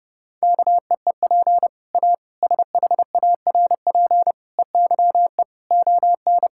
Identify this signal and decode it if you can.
Morse code, heard as a keyed tone, about 30 wpm, 700 Hz. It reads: KEEP A SHARP EYE ON
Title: Sentences from Top 1600 Words 30wpm